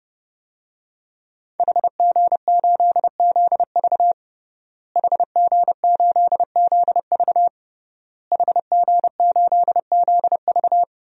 Morse code, heard as a keyed tone, about 30 wpm, 700 Hz. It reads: HG8ZV HG8ZV HG8ZV
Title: DXCC-100 Call Signs - 5 Characters - Repeated 3x - 30wpm